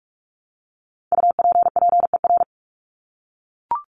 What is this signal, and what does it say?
Morse code, heard as a keyed tone, about 44 wpm, 700 Hz. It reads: UPPER
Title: Top 1751-2000 Words 45wpm